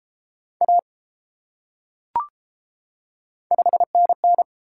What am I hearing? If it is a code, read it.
Morse code, heard as a keyed tone, about 30 wpm, 700 Hz. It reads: A 5NN